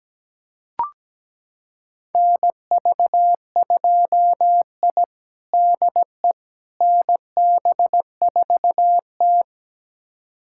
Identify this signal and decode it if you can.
Morse code, heard as a keyed tone, about 17 wpm, 700 Hz. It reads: NV2I DE NB4T
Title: USA Callsign DE Callsign - 4 Characters - 17wpm